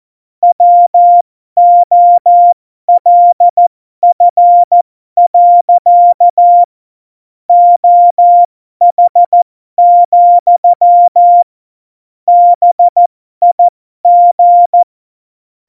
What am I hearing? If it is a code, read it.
Morse code, heard as a keyed tone, about 14 wpm, spaced as for 12 wpm, 700 Hz. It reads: WOLF. OH, BIG
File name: Morsekurs / Witze